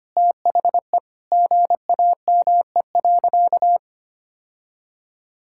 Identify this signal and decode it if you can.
Morse code, heard as a keyed tone, about 25 wpm, 700 Hz. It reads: THE GAME.
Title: Sentences from Top 1000 Words - Review Part 6 - 25wpm